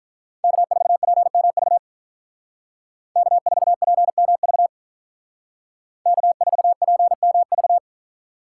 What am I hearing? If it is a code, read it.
Morse code, heard as a keyed tone, about 45 wpm, 700 Hz. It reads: K4PMV K4PMV K4PMV